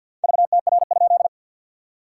Morse code, heard as a keyed tone, about 50 wpm, 690 Hz. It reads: VTR?
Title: SOTA QSO Practice 50wpm